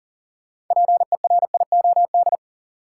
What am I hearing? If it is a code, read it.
Morse code, heard as a keyed tone, about 40 wpm, 700 Hz. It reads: PERIOD
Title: Sets of 5 Words - New Word Drill - 40wpm